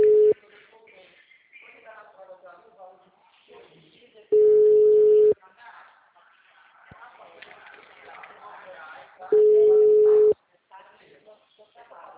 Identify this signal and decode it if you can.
Ringback tone (Europe, 425 Hz), with speech.
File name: out- -2018- - - .15043